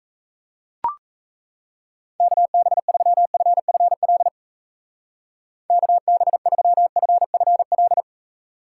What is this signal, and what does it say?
Morse code, heard as a keyed tone, about 33 wpm, 700 Hz. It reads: KB3FFL KB3FFL
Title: Speed Racing - USA 2x3 Call Signs 28wpm